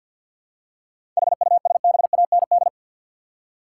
Morse code, heard as a keyed tone, about 47 wpm, 700 Hz. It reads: HUSBAND
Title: Seven Letter Words - 50wpm